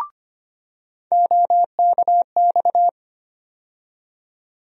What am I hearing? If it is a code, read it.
Morse code, heard as a keyed tone, about 25 wpm, 700 Hz. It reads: OKX
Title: USA Call Sign Postfixes - 3 Letters 25wpm